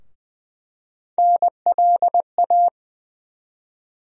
Morse code, heard as a keyed tone, about 20 wpm, 700 Hz. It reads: NLA
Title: USA Call Sign Postfixes - 3 Letters 20wpm